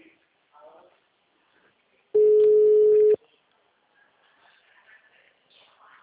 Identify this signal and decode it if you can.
Ringback tone (Europe, 425 Hz), with speech.